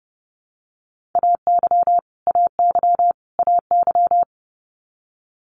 Morse code, heard as a keyed tone, about 30 wpm, 700 Hz. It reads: AY AY AY
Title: Multi-letter Phonemes Repeated 3x 30wpm